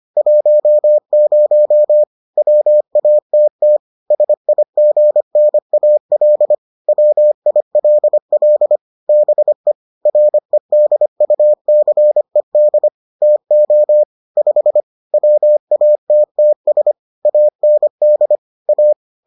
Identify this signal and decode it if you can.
Morse code, heard as a keyed tone, about 25 wpm, 600 Hz. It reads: 10 WATT SIGNAL WILL BE REDUCED TO 5 WATTS AND A